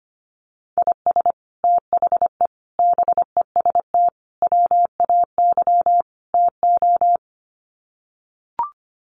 Morse code, heard as a keyed tone, about 25 wpm, 700 Hz. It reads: IS THE BEST WAY TO
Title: Sets of 5 Words Encore 25wpm